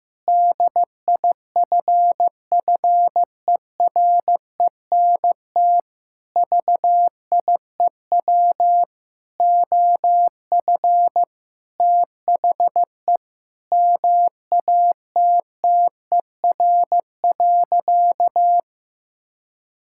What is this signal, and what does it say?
Morse code, heard as a keyed tone, about 15 wpm, 700 Hz. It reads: DIFFERENT VIEW OF THE MATTER.
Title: Sentences from Top 1600 Words 15wpm